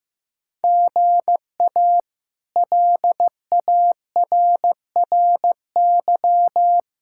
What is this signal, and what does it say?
Morse code, heard as a keyed tone, about 15 wpm, 700 Hz. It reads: GA LARRY
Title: K1USN SST QSO Practice - 15wpm